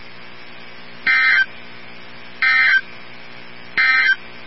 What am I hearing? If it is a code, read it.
Emergency Alert System (SAME) tones, heard as harsh buzzing bursts. It is heard from 0.9 to 4.2 s.
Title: NOAA Weather Radio 162.550